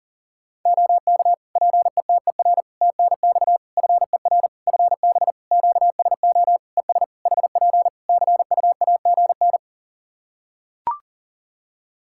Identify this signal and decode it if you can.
Morse code, heard as a keyed tone, about 40 wpm, 700 Hz. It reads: OK PETER TNX FER FB QSO ES HP CUAGN
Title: Practice QSO - Exchange 4 - Ending - 40wpm